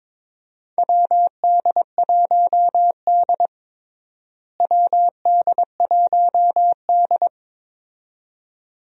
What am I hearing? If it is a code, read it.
Morse code, heard as a keyed tone, about 21 wpm, 700 Hz. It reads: WD1D WD1D
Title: USA Call Signs - 4 Characters - Repeated 2x - 22wpm